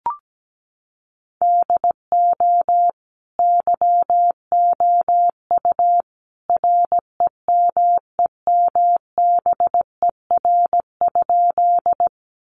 Morse code, heard as a keyed tone, about 17 wpm, 700 Hz. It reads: DO YOU REMEMBER?
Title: Sentences from Top 400 Words 17wpm